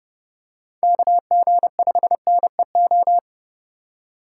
Morse code, heard as a keyed tone, about 30 wpm, 700 Hz. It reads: KG5NEO